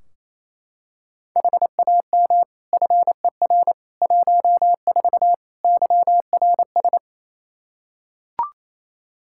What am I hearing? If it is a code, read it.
Morse code, heard as a keyed tone, about 28 wpm, 700 Hz. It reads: HAM FER 14 YRS
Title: K1USN SST QSO Practice 28wpm